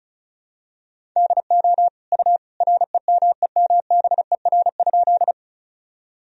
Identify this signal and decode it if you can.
Morse code, heard as a keyed tone, about 36 wpm, 700 Hz. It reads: DO U REMEMBER?
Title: Sentences from Top 400 Words and Abbreviations 35wpm